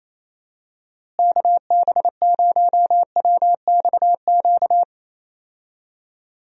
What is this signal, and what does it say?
Morse code, heard as a keyed tone, about 28 wpm, 700 Hz. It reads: KB0WXQ
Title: USA Call Signs - 6 Characters - Repeated 2x - 28wpm